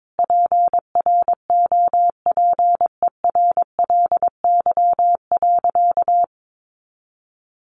Morse code, heard as a keyed tone, about 22 wpm, 700 Hz. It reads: PROPERLY.
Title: Sentences from Top 1100 Words 22wpm